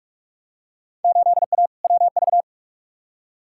Morse code, heard as a keyed tone, about 42 wpm, 700 Hz. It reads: 8A WV
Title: Field Day Exchanges Encore - 45wpm